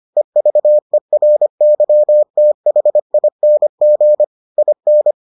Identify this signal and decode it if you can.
Morse code, heard as a keyed tone, about 25 wpm, 600 Hz. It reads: EVERYTHING IN